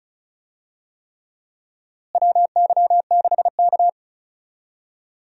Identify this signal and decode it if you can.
Morse code, heard as a keyed tone, about 36 wpm, 700 Hz. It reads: WY6K
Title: USA Call Signs - 4 Characters - Repeated 3x - 35wpm